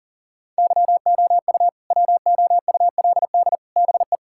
Morse code, heard as a keyed tone, about 40 wpm, 700 Hz. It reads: YOU WOULD BE
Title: Sentences from Top 1000 Words - Review Part 6 - 40wpm